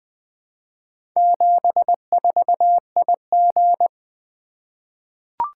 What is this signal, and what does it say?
Morse code, heard as a keyed tone, about 20 wpm, 700 Hz. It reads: 74IG
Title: Four Letter-Number Combinations - 20wpm